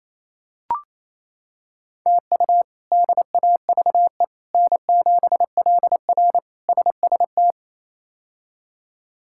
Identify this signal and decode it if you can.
Morse code, heard as a keyed tone, about 28 wpm, 700 Hz. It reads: TU DAVE N7LR SST
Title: K1USN SST QSO Practice - 28wpm